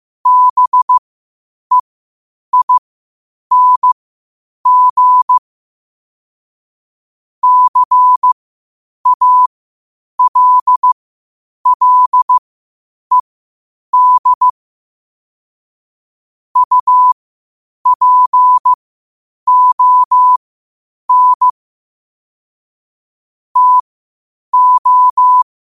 Morse code, heard as a keyed tone, about 15 wpm, spaced as for 8 wpm, 1000 Hz. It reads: BEING CALLED UPON TO
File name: qotdfiles / 8wpm